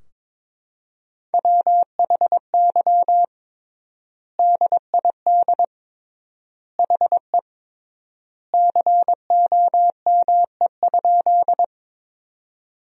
Morse code, heard as a keyed tone, about 22 wpm, 700 Hz. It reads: WHY DID HE COME?
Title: 3x Word-Spacing - Sentences from Top 200 Words - 22wpm